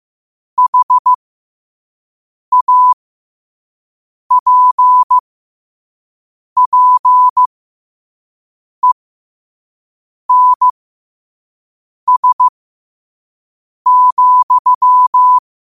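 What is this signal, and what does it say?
Morse code, heard as a keyed tone, about 15 wpm, spaced as for 5 wpm, 1000 Hz. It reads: HAPPENS,
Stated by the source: Morse code practice at 5 wpm.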